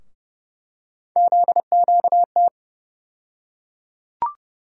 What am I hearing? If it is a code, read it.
Morse code, heard as a keyed tone, about 31 wpm, 700 Hz. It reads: ZQT
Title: USA Call Sign Postfixes - 3 Letters 30wpm